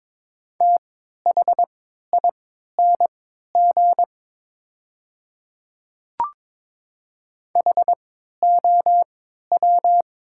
Morse code, heard as a keyed tone, about 23 wpm, spaced as for 13 wpm, 700 Hz. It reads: THING HOW
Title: 3x Character Spacing - Top 300 Words - 22wpm